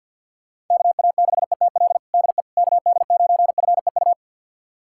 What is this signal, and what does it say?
Morse code, heard as a keyed tone, about 53 wpm, 690 Hz. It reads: KA6ETL DE KD9FEU
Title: USA Callsign DE Callsign - 6 Characters - 50wpm